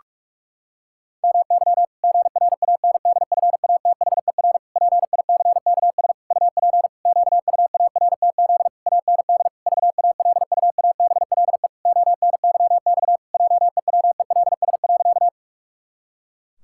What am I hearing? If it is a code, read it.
Morse code, heard as a keyed tone, about 41 wpm, 700 Hz. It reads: MY GRANDFATHER PICKS UP QUARTZ AND VALUABLE ONYX JEWELS.
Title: Pangram Challenge v2 - 45wpm